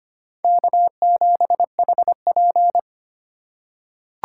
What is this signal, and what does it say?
Morse code, heard as a keyed tone, about 25 wpm, 690 Hz. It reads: K7HP
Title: USA Call Signs - 4 Characters - 4x Character Spacing - 25wpm